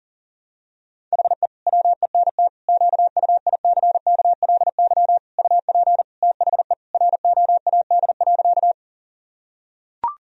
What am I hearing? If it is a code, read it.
Morse code, heard as a keyed tone, about 40 wpm, 700 Hz. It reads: HE WENT QUICKLY UP THE ROAD.
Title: Sentences from Top 1000 Words - Review Part 2 - 40wpm